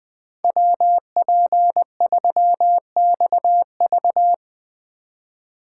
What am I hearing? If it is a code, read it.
Morse code, heard as a keyed tone, about 21 wpm, 700 Hz. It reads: WP3XV